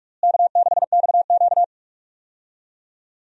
Morse code, heard as a keyed tone, about 46 wpm, 700 Hz. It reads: K6XQ
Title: USA Call Signs - 4 Characters - Repeated 3x - 45wpm